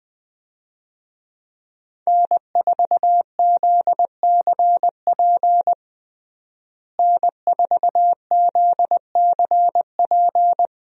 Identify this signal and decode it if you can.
Morse code, heard as a keyed tone, about 20 wpm, 700 Hz. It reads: N4ZCP N4ZCP
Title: USA Call Signs - 5 Characters - Repeated 2x - 20wpm